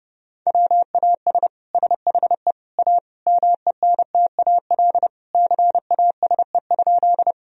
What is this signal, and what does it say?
Morse code, heard as a keyed tone, about 30 wpm, 700 Hz. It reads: WAS SHE A MENTAL CASE?
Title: Sentences from Top 1000 Words - Review Part 4 - 30wpm